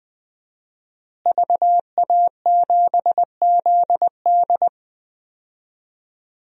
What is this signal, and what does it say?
Morse code, heard as a keyed tone, about 20 wpm, 700 Hz. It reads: VA7ZD
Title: DXCC-100 Call Signs - 5 Characters - Repeated 3x - 20wpm